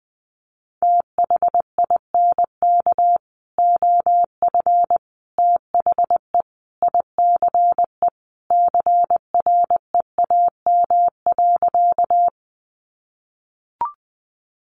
Morse code, heard as a keyed tone, about 19 wpm, 700 Hz. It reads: THINK OF THE ICE CREAM.
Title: Sentences from Top 1500 Words 20wpm